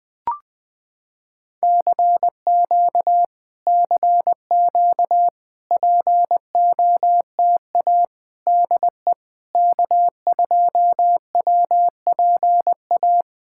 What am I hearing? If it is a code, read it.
Morse code, heard as a keyed tone, about 20 wpm, 700 Hz. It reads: CQ CQ POTA DE K2WPA
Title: POTA QSO Practice - 20wpm